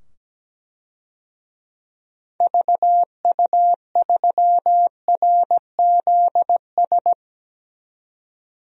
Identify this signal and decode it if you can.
Morse code, heard as a keyed tone, about 17 wpm, 700 Hz. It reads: VU3RZS
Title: DXCC-100 Call Signs - 6 Characters - 17wpm